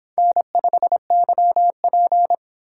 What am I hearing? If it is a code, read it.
Morse code, heard as a keyed tone, about 26 wpm, 700 Hz. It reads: N5YP